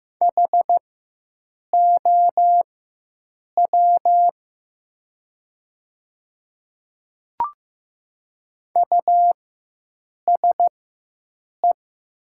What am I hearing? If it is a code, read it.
Morse code, heard as a keyed tone, about 15 wpm, spaced as for 7 wpm, 700 Hz. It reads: HOW USE
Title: 4x Character Spacing - Top 100 Words - 15wpm